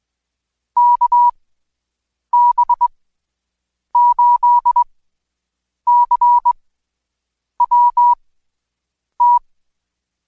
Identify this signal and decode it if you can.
Morse code, heard as a keyed tone, about 20 wpm, spaced as for 7 wpm, 970 Hz. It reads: KB8CWT
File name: Code Quick / AudioDownld2 / Seg112-Day17-DayDrill 17